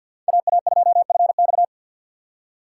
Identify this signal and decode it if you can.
Morse code, heard as a keyed tone, about 48 wpm, 700 Hz. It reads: AA2FX
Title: USA Call Signs - 5 Characters - 3x Character Spacing - 50wpm